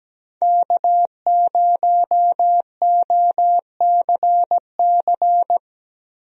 Morse code, heard as a keyed tone, about 17 wpm, 700 Hz. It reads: K0OCC
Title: USA Call Signs - 5 Characters - Repeated 2x - 17wpm